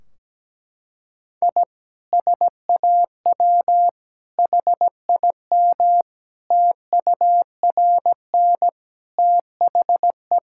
Morse code, heard as a keyed tone, about 17 wpm, 700 Hz. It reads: I SAW HIM TURN THE
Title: Sentences from Top 1600 Words 17wpm